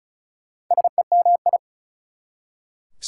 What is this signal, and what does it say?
Morse code, heard as a keyed tone, about 48 wpm, spaced as for 41 wpm, 690 Hz. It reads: SEMI
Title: Speed-Racing Prefixes and Suffixes 35wpm